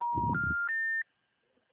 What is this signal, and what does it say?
Special information tone.
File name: out- -2001- - - .4